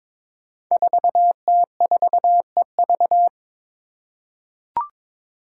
Morse code, heard as a keyed tone, about 21 wpm, 700 Hz. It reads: 4T4EV
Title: Five Challenging Letter-Number Combinations 22wpm